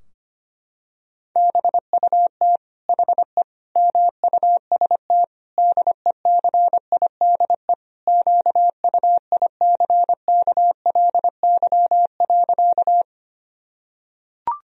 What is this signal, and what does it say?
Morse code, heard as a keyed tone, about 25 wpm, 700 Hz. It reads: BUT HE MUST DECIDE QUICKLY.
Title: Sentences from Top 1000 Words - Review Part 3 - 25wpm